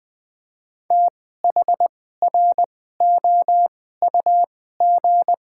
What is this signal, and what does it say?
Morse code, heard as a keyed tone, about 20 wpm, spaced as for 14 wpm, 700 Hz. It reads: THROUG
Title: 2x Character Spacing - Top 300 Words - 20wpm